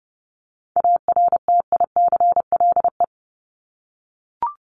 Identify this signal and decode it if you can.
Morse code, heard as a keyed tone, about 30 wpm, 700 Hz. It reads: ARTICLE
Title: Top 751-1000 Words 30wpm